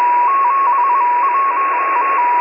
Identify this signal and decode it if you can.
Morse code, heard as a keyed tone, about 22 wpm, 1100 Hz, with radio noise. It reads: 7 1